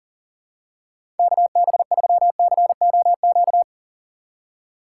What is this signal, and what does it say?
Morse code, heard as a keyed tone, about 40 wpm, 700 Hz. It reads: KB3COQ
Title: USA Call Signs - 6 Characters - 4x Character Spacing - 40wpm